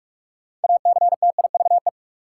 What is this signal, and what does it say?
Morse code, heard as a keyed tone, about 41 wpm, 700 Hz. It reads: ACTIVE
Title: Sentences from Top 900 Words - New Word Drill - 45wpm